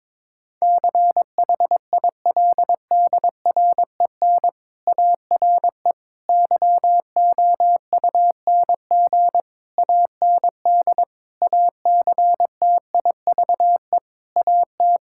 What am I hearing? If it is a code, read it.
Morse code, heard as a keyed tone, about 22 wpm, 700 Hz. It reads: CHILDREN ARE YOUNG AND ACTIVE AT
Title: Sentences from Top 1000 Words - Review Part 6 - 22wpm